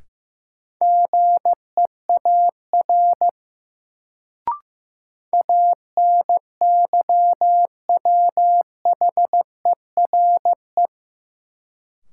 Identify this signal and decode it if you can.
Morse code, heard as a keyed tone, about 16 wpm, 700 Hz. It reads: GEAR ANYWHERE
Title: Sentences from Top 1400 Words - New Word Drill Alt - 15wpm